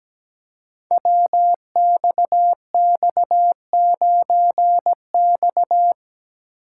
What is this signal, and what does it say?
Morse code, heard as a keyed tone, about 17 wpm, 700 Hz. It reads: WXX9X
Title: Code Groups - 5 Letter:Numbers - 17wpm